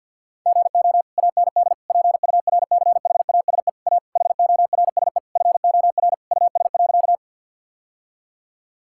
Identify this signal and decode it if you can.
Morse code, heard as a keyed tone, about 50 wpm, 700 Hz. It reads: GO AND PURCHASE A HORSE FOR US.
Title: Sentences from Top 1000 Words - Review Part 5 - 50wpm